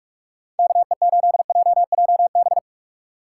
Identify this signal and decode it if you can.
Morse code, heard as a keyed tone, about 41 wpm, 700 Hz. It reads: KE8JJB
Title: USA Call Signs - 6 Characters - 3x Character Spacing - 45wpm